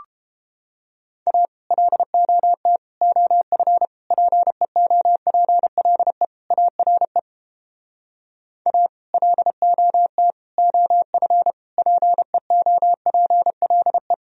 Morse code, heard as a keyed tone, about 29 wpm, 700 Hz. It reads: A LOT OF PEOPLE ARE A LOT OF PEOPLE